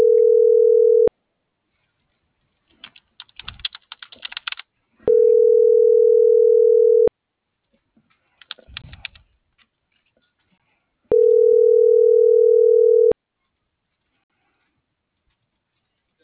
Ringback tone (North America), 440 + 480 Hz.